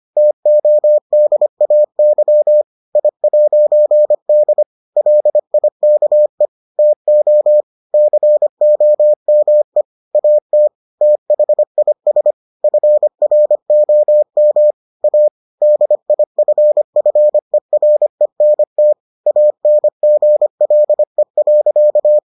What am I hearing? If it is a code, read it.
Morse code, heard as a keyed tone, about 25 wpm, 600 Hz. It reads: TODAY I'D LIKE TO COME AT THIS FROM A DIFFERENT ANGLE.